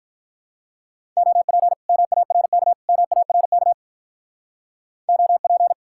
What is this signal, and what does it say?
Morse code, heard as a keyed tone, about 43 wpm, 700 Hz. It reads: OP MARK MARK OP
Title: Speed-Racing Common QSO Phrases - 35wpm